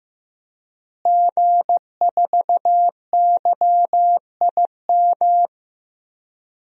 Morse code, heard as a keyed tone, about 15 wpm, 700 Hz. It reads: G4YIM
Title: DXCC-100 Call Signs - 5 Characters - Repeated 3x - 15wpm